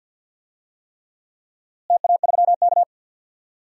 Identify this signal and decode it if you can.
Morse code, heard as a keyed tone, about 48 wpm, 700 Hz. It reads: TA3K
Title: DXCC-100 Call Signs - 4 Characters - Repeated 3x - 50wpm